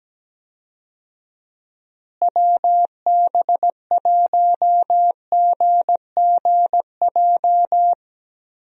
Morse code, heard as a keyed tone, about 17 wpm, 700 Hz. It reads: WB1GGJ
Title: USA Call Signs - 6 Characters - Repeated 2x - 17wpm